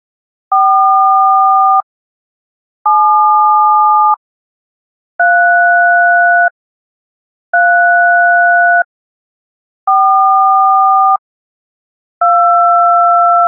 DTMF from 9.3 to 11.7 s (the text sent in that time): 4